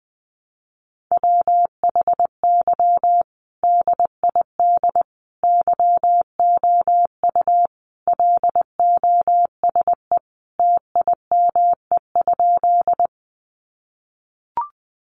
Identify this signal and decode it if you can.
Morse code, heard as a keyed tone, about 20 wpm, 700 Hz. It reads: WHY DID YOU LOSE TIME?
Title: Sentences from Top 1100 Words 20wpm